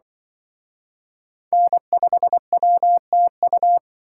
Morse code, heard as a keyed tone, about 23 wpm, 700 Hz. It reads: N5WTU
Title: Speed Racing - USA 1x3 Call Signs 20wpm